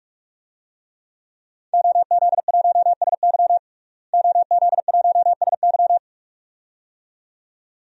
Morse code, heard as a keyed tone, about 48 wpm, 700 Hz. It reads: OZ1SY OZ1SY
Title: DXCC-100 Call Signs - 5 Characters - Repeated 2x - 45wpm